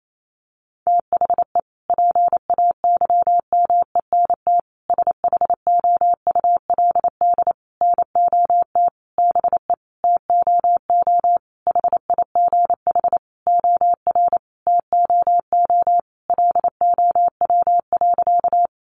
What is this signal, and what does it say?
Morse code, heard as a keyed tone, about 28 wpm, 700 Hz. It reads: THE PAYMENT SHOULD NOT BE TOO HIGH OR TOO LOW.
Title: Sentences from Top 1200 Words 28wpm